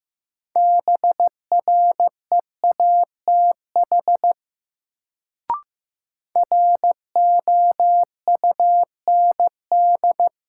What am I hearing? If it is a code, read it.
Morse code, heard as a keyed tone, about 15 wpm, 700 Hz. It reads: BREATH ROUND
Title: Sets of 5 Words Encore - New Word Drill - 15wpm